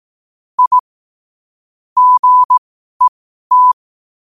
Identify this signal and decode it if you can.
Morse code, heard as a keyed tone, about 18 wpm, spaced as for 12 wpm, 1000 Hz. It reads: I GET
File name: qotdfiles / 12wpm